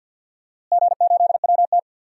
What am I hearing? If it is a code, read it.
Morse code, heard as a keyed tone, about 47 wpm, 700 Hz. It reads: G8WT
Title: DXCC-100 Call Signs - 4 Characters - Repeated 2x - 50wpm